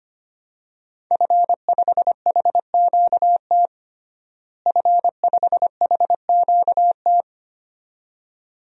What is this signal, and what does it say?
Morse code, heard as a keyed tone, about 25 wpm, 700 Hz. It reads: F5HQT F5HQT
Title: DXCC-100 Call Signs - 5 Characters - Repeated 2x - 25wpm